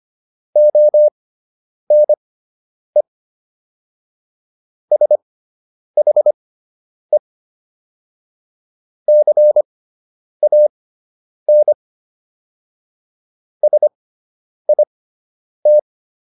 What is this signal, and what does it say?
Morse code, heard as a keyed tone, about 25 wpm, spaced as for 9 wpm, 600 Hz. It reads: ONE SHE CAN SIT